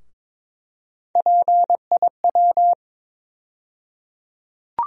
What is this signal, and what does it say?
Morse code, heard as a keyed tone, about 22 wpm, 700 Hz. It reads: PIW
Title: USA Call Sign Postfixes - 3 Letters 22wpm